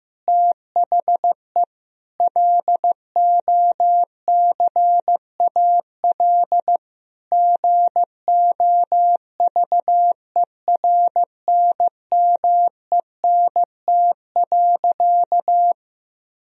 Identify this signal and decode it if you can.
Morse code, heard as a keyed tone, about 15 wpm, 700 Hz. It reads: THE LOCAL GOVERNMENT.
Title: Sentences from Top 1000 Words - Review Part 3 - 15wpm